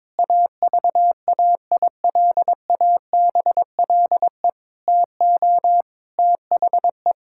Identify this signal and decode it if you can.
Morse code, heard as a keyed tone, about 22 wpm, 700 Hz. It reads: AVAILABLE TO THE
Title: Sentences from Top 1000 Words - Review Part 6 - 22wpm